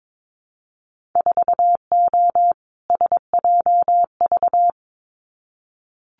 Morse code, heard as a keyed tone, about 22 wpm, 700 Hz. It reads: 4O SJV
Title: Winter Field Day Exchanges Encore 22wpm